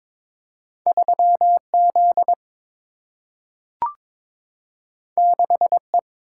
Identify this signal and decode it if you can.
Morse code, heard as a keyed tone, about 22 wpm, 700 Hz. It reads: 3Z 6E
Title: Single-Letter Single-Number 22wpm